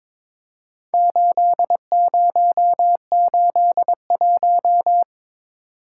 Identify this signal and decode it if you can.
Morse code, heard as a keyed tone, about 23 wpm, 700 Hz. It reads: 8081
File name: practice / 22wpm / Four Letter-Number Combinations - 22wpm